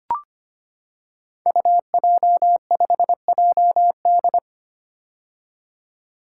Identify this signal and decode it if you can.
Morse code, heard as a keyed tone, about 25 wpm, 700 Hz. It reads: UJ5JD
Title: Code Groups - 5 Letter:Numbers - 25wpm